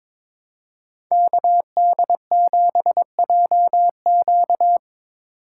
Morse code, heard as a keyed tone, about 22 wpm, 700 Hz. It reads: KD7JQ
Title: USA Call Signs - 5 Characters - 4x Character Spacing - 22wpm